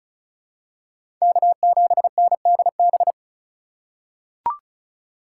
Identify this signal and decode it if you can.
Morse code, heard as a keyed tone, about 36 wpm, 700 Hz. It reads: K7NDB
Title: USA Call Signs - 8x Character Spacing - 35wpm